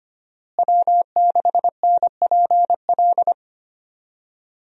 Morse code, heard as a keyed tone, about 25 wpm, 700 Hz. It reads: W6NPL
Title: USA Call Signs - 5 Characters - Repeated 3x - 25wpm